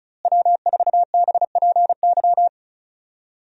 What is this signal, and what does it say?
Morse code, heard as a keyed tone, about 36 wpm, 700 Hz. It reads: W4BPY
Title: USA Call Signs - 5 Characters - Repeated 3x - 35wpm